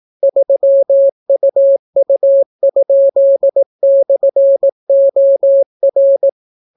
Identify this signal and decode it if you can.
Morse code, heard as a keyed tone, about 18 wpm, 550 Hz. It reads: 3UU?/OR